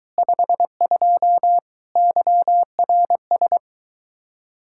Morse code, heard as a keyed tone, about 23 wpm, 700 Hz. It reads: 52 YRS